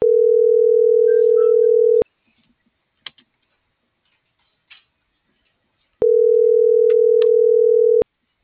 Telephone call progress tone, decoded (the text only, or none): ringback tone